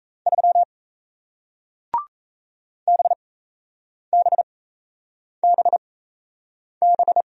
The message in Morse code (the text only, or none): 3 BBBB